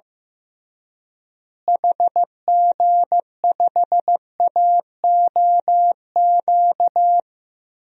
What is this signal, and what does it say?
Morse code, heard as a keyed tone, about 15 wpm, 700 Hz. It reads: HG5AOQ
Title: DXCC-100 Call Signs - 6 Characters - Repeated 3x - 15wpm